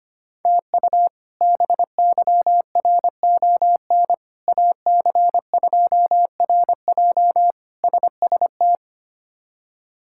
Morse code, heard as a keyed tone, about 25 wpm, 700 Hz. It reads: TU BYRON AC2RJ SST
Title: K1USN SST QSO Practice - 25wpm